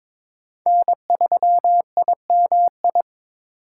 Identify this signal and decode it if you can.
Morse code, heard as a keyed tone, about 21 wpm, 700 Hz. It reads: N3IMI
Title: USA Call Signs - 4x Character Spacing - 22wpm